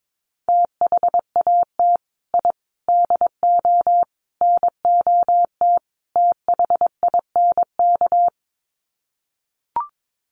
Morse code, heard as a keyed tone, about 22 wpm, 700 Hz. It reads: THAT I DO NOT THINK
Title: Sets of 5 Words Encore 22wpm